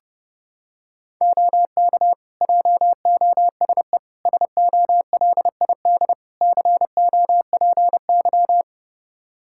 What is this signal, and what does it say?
Morse code, heard as a keyed tone, about 30 wpm, 700 Hz. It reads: OK JOSE SOLID COPY
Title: Practice QSO - Exchange 3 - More Info - 30wpm